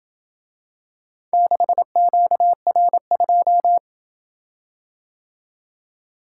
Morse code, heard as a keyed tone, about 26 wpm, 700 Hz. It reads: 6QR2